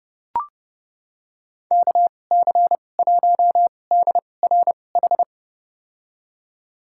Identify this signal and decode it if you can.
Morse code, heard as a keyed tone, about 29 wpm, spaced as for 21 wpm, 700 Hz. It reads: KC1DRH
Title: USA Call Signs - 6 Characters - 2x Character Spacing - 30wpm